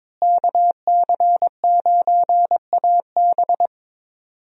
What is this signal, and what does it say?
Morse code, heard as a keyed tone, about 22 wpm, 700 Hz. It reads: KC9AB
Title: USA Call Signs - 5 Characters - Repeated 3x - 22wpm